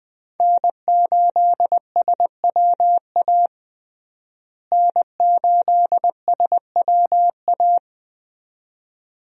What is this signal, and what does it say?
Morse code, heard as a keyed tone, about 21 wpm, 700 Hz. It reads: N8SWA N8SWA
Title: USA Call Signs - 5 Characters - Repeated 2x - 20wpm